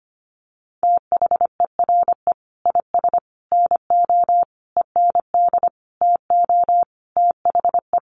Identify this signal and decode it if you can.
Morse code, heard as a keyed tone, about 25 wpm, 700 Hz. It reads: THERE IS NO END TO THE
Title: Sentences from Top 1200 Words 25wpm